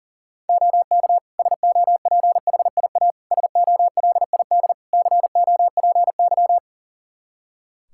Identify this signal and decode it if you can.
Morse code, heard as a keyed tone, about 40 wpm, 700 Hz. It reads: OK SOPHIA SOLID COPY
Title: Practice QSO - Exchange 3 - More Info - 40wpm